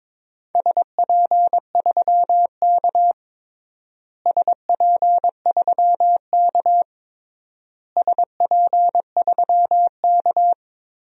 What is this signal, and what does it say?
Morse code, heard as a keyed tone, about 22 wpm, 700 Hz. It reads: SP3K SP3K SP3K
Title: DXCC-100 Call Signs - 4 Characters - Repeated 3x - 22wpm